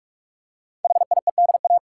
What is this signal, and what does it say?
Morse code, heard as a keyed tone, about 44 wpm, 700 Hz. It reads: HIEDA
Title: Code Groups - 5 Letters - 45wpm